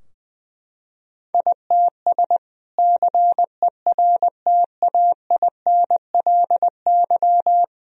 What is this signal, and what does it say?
Morse code, heard as a keyed tone, about 21 wpm, 700 Hz. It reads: ITS CERTAINLY
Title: Sentences from Top 1000 Words - Review Part 2 - 20wpm